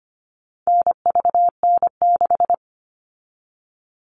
Morse code, heard as a keyed tone, about 25 wpm, 700 Hz. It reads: NVN6
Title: All Cut Numbers - 4-Digit 25wpm